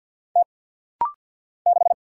Morse code, heard as a keyed tone, about 50 wpm, 690 Hz. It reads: T 6
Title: Single Letter-Number Alt - Rapid-Fire 50wpm